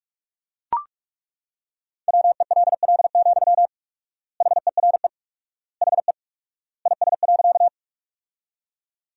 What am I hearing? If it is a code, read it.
Morse code, heard as a keyed tone, about 43 wpm, 700 Hz. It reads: WELL, HERE HE IS.
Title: 4x Word-Spacing - Sentences from Top 200 Words - 45wpm